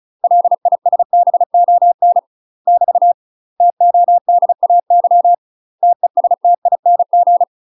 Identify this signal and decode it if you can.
Morse code, heard as a keyed tone, about 35 wpm, 700 Hz. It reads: LISBON <BT> TODAY TESTING
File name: UK MASTER QSOs TRAINING / 35WPM / UK MASTER QSOs 0043